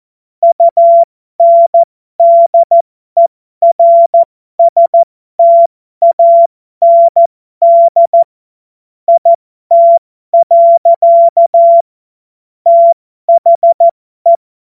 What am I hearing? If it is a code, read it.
Morse code, heard as a keyed tone, about 14 wpm, spaced as for 12 wpm, 680 Hz. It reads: UNDERSTAND IT. THE